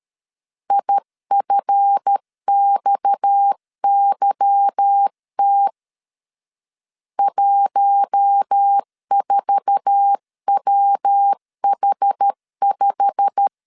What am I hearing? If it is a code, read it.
Morse code, heard as a keyed tone, about 13 wpm, 800 Hz. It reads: IFXYT 14WH5
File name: LEZIONI / 8 YJ45 / 8 YJ45 12wpm 2151